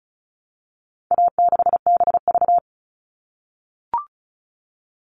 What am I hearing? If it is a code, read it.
Morse code, heard as a keyed tone, about 34 wpm, 700 Hz. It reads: A6BV
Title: All Cut Numbers - 4-Digit 35wpm